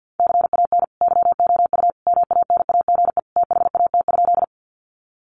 Morse code, heard as a keyed tone, about 50 wpm, 700 Hz. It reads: CAN YOU MANAGE THAT?
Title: Sentences from Top 1400 Words 50wpm